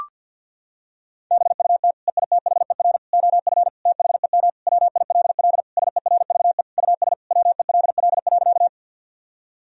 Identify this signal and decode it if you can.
Morse code, heard as a keyed tone, about 50 wpm, 700 Hz. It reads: BUT EITHER OF THEM WILL SERVE US WELL.
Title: Sentences from Top 1000 Words - Review Part 4 - 50wpm